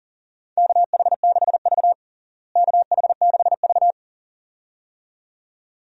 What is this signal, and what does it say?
Morse code, heard as a keyed tone, about 40 wpm, 700 Hz. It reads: KH6V KH6V
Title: DXCC-100 Call Signs - 4 Characters - Repeated 2x - 40wpm